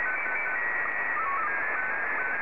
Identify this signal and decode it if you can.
SSTV (slow-scan television), Scottie 1.